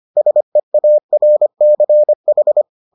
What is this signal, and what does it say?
Morse code, heard as a keyed tone, about 25 wpm, 600 Hz. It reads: SEARCH